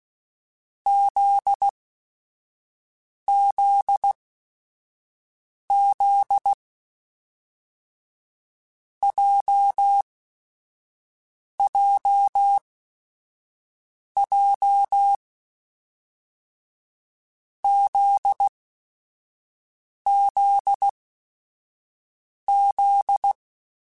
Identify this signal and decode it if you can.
Morse code, heard as a keyed tone, about 16 wpm, spaced as for 5 wpm, 800 Hz. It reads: ZZZ JJJ ZZZ
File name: Groups@5 / Groups@5 mp3s / JZ17@5